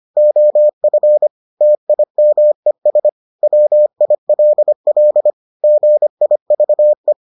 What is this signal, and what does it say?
Morse code, heard as a keyed tone, about 25 wpm, 600 Hz. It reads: OF TIMES WILL GIVE